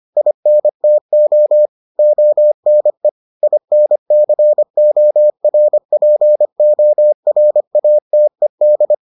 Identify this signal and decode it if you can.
Morse code, heard as a keyed tone, about 25 wpm, 600 Hz. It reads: INTO ONE INCORPORATED